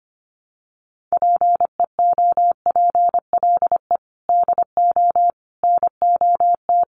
Morse code, heard as a keyed tone, about 25 wpm, 700 Hz. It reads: PEOPLE DO NOT
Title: Sets of 5 Words Encore2 25wpm